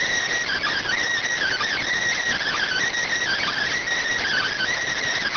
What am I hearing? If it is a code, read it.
Emergency Alert System (SAME) tones (header bursts), heard as harsh buzzing bursts. It is heard from 0.4 to 4.7 s.